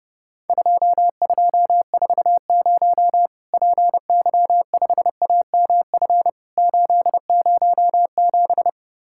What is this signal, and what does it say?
Morse code, heard as a keyed tone, about 30 wpm, 700 Hz. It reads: 2240 PY5AMF 807
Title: Worked All Europe DX Contest - QTC Exchange - 30wpm